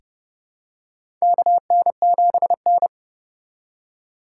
Morse code, heard as a keyed tone, about 32 wpm, 700 Hz. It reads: KN7N
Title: USA Call Signs - 4 Characters - 2x Character Spacing - 30wpm